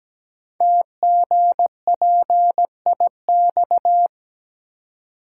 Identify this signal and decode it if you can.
Morse code, heard as a keyed tone, about 17 wpm, 700 Hz. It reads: TGPIX
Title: Code Groups - 5 Letters - 17wpm